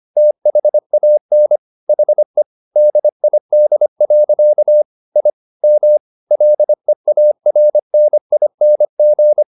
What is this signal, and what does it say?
Morse code, heard as a keyed tone, about 25 wpm, 600 Hz. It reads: THAN HE DID. I M LEARNING